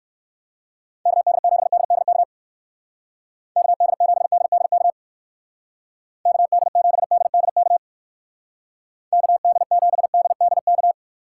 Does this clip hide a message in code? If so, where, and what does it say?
Yes: Morse code from 1.1 to 10.9 s, heard as a keyed tone, about 59 wpm, 700 Hz. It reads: KD7DDK KD7DDK KD7DDK KD7DDK